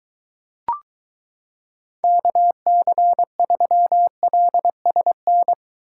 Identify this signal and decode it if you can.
Morse code, heard as a keyed tone, about 23 wpm, 700 Hz. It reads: KC3LSN